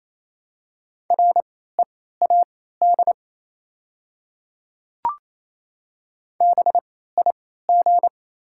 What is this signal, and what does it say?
Morse code, heard as a keyed tone, about 28 wpm, spaced as for 16 wpm, 700 Hz. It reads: READ BIG
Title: 3x Character Spacing - Top 300 Words - 28wpm